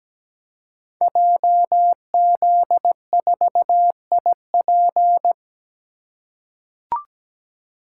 Morse code, heard as a keyed tone, about 17 wpm, 700 Hz. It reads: JZ4IP
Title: Code Groups - 5 Letter:Numbers - 17wpm